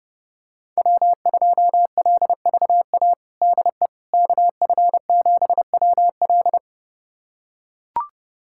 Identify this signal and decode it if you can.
Morse code, heard as a keyed tone, about 32 wpm, 700 Hz. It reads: W2LVA DE KF7WL
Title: USA Callsign DE Callsign - 5 Characters - 30wpm